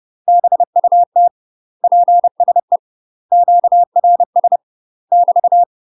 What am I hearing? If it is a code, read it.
Morse code, heard as a keyed tone, about 30 wpm, 700 Hz. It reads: BUT PSE QRS <BT>
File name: UK MASTER QSOs TRAINING / 30WPM / UK MASTER QSOs 0036